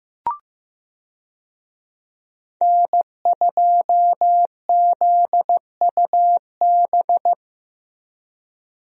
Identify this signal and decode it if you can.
Morse code, heard as a keyed tone, about 15 wpm, 700 Hz. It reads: N2ZUB
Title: USA Call Signs - 5 Characters - Repeated 3x - 15wpm